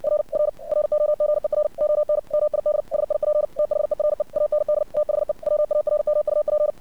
Morse code, heard as a keyed tone, about 60 wpm, 600 Hz. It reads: R R ROGER ON MID SIP THERE AND THE WARMUP
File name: archived / CRSnet